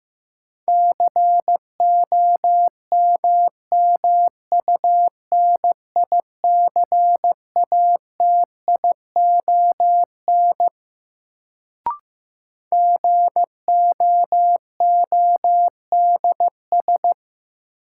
Morse code, heard as a keyed tone, about 15 wpm, 700 Hz. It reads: COMMUNICATION GOODS
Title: Sentences from Top 1000 Words - Review Part 6 - New Word Drill - 15wpm